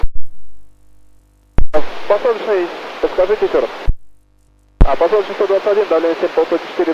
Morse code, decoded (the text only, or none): none